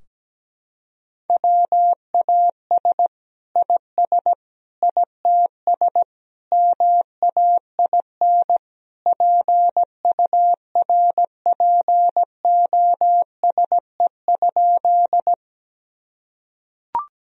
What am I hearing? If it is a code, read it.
Morse code, heard as a keyed tone, about 17 wpm, 700 Hz. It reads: WAS IS ITS MAIN PURPOSE?
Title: Sentences from Top 1000 Words - Review Part 2 - 17wpm